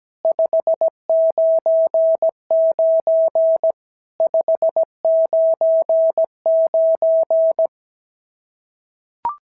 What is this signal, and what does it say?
Morse code, heard as a keyed tone, about 17 wpm, 650 Hz. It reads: 599 599
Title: SOTA QSO Practice 17wpm